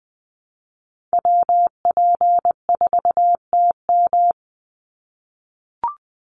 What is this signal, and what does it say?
Morse code, heard as a keyed tone, about 20 wpm, 700 Hz. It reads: WP4TM
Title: USA Call Signs 20wpm